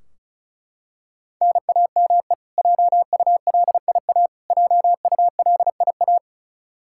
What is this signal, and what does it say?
Morse code, heard as a keyed tone, about 34 wpm, 700 Hz. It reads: NAME JULIA JULIA
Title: Practice QSO - Exchange 1 - Beginning - 35wpm